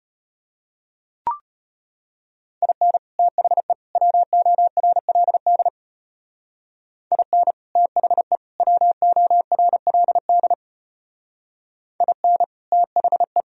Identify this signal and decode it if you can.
Morse code, heard as a keyed tone, about 33 wpm, 700 Hz. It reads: IN THE WORLD IN THE WORLD IN THE